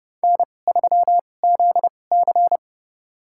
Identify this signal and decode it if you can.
Morse code, heard as a keyed tone, about 30 wpm, spaced as for 22 wpm, 700 Hz. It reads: N3ZC
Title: USA Call Signs - 4 Characters - 2x Character Spacing - 30wpm